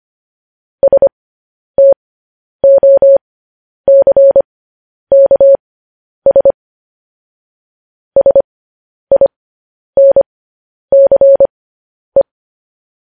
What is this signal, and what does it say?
Morse code, heard as a keyed tone, about 25 wpm, spaced as for 10 wpm, 550 Hz. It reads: STOCKS SINCE
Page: news headlines in morse code at 10 wpm